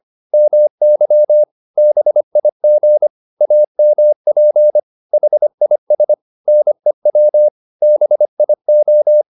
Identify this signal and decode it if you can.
Morse code, heard as a keyed tone, about 25 wpm, 600 Hz. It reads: MY BIG AMP HIS NEW BIO